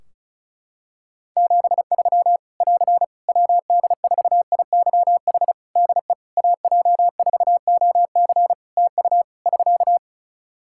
Morse code, heard as a keyed tone, about 34 wpm, 700 Hz. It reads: 73 <AR> WD4IYH DE AJ4OC TU <SK>
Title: Practice QSO - Exchange 4 - Ending - 35wpm